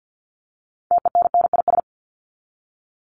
Morse code, heard as a keyed tone, about 50 wpm, 700 Hz. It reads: TENNIS
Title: Top 1501-1750 Words 50wpm